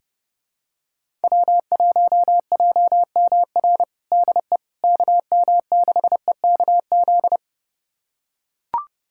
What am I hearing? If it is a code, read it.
Morse code, heard as a keyed tone, about 32 wpm, 700 Hz. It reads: W1JMR DE KM6EKZ
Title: USA Callsign DE Callsign 30wpm